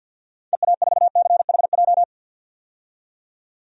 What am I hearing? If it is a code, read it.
Morse code, heard as a keyed tone, about 48 wpm, 700 Hz. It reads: EA4CHJ
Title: DXCC-100 Call Signs - 6 Characters - 50wpm